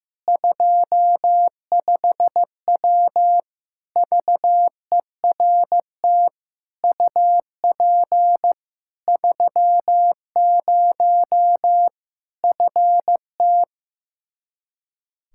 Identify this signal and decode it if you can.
Morse code, heard as a keyed tone, about 15 wpm, 700 Hz. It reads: 25W VERT UP 30 FT
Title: Practice QSO - Exchange 2 - Info - 15wpm